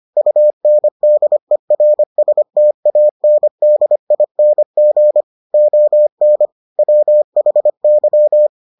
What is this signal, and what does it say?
Morse code, heard as a keyed tone, about 25 wpm, 600 Hz. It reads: UNDERSTANDING ON WHY